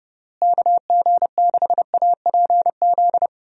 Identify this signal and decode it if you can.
Morse code, heard as a keyed tone, about 32 wpm, 700 Hz. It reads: KG6APZ
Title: USA Call Signs - 6 Characters - Repeated 3x - 30wpm